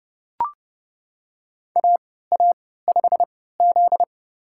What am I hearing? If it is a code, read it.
Morse code, heard as a keyed tone, about 30 wpm, spaced as for 17 wpm, 700 Hz. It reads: AA5Z
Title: USA Call Signs - 3x Character Spacing - 30wpm